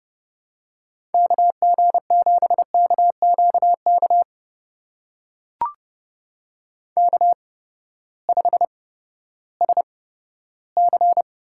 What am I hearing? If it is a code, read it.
Morse code, heard as a keyed tone, about 32 wpm, spaced as for 15 wpm, 700 Hz. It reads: KG7KQK K5SC
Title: USA Call Signs - 8x Character Spacing - 30wpm